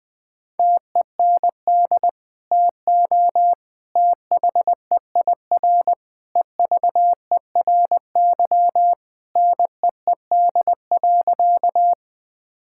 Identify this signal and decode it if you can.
Morse code, heard as a keyed tone, about 20 wpm, 700 Hz. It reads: TEND TO THEIR EVERY NEED.
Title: Sentences from Top 1000 Words - Review Part 5 - 20wpm